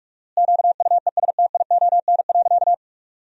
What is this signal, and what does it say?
Morse code, heard as a keyed tone, about 46 wpm, 700 Hz. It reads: QUESTION.
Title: 4x Word-Spacing - Sentences from Top 400 Words - 45wpm